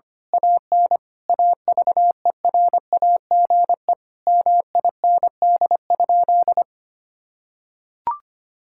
Morse code, heard as a keyed tone, about 25 wpm, 700 Hz. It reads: AN AVERAGE MIND?
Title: Sentences from Top 1000 Words - Review Part 6 - 25wpm